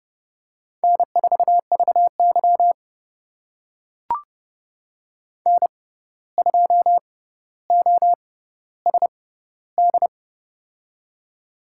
Morse code, heard as a keyed tone, about 30 wpm, spaced as for 14 wpm, 700 Hz. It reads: N4VY N2OSD
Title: USA Call Signs - 6x Character Spacing - 30wpm